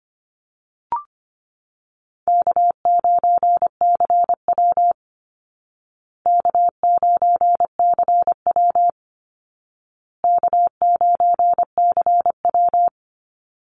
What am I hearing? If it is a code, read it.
Morse code, heard as a keyed tone, about 25 wpm, 700 Hz. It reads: K9CW K9CW K9CW
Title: Call Signs Repeated 2x - 25wpm